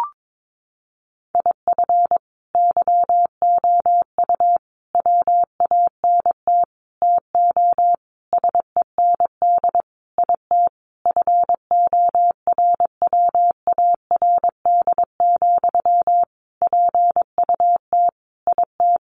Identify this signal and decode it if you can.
Morse code, heard as a keyed tone, about 22 wpm, 700 Hz. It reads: IF YOU WANT TO SEND IT FORWARD, PUT IT
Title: Sentences from Top 1500 Words 22wpm